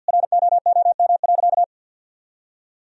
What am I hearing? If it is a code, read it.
Morse code, heard as a keyed tone, about 50 wpm, 700 Hz. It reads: ROOM.
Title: Sentences from Top 1000 Words - Review Part 4 - 50wpm